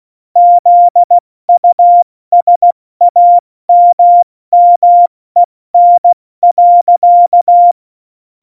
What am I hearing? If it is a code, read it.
Morse code, heard as a keyed tone, about 16 wpm, 700 Hz. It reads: ZUSAMMEN.
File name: Morsekurs / Witze / Witz 29. Dezember 2020, 70 F80